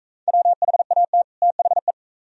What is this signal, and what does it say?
Morse code, heard as a keyed tone, about 41 wpm, 700 Hz. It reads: WHAT THE